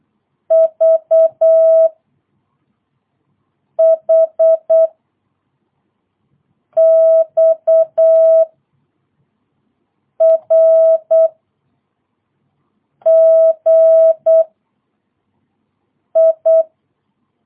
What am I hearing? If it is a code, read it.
Morse code, heard as a keyed tone, about 8 wpm, 650 Hz. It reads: V H X R G I
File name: jh8cbh / 2022cw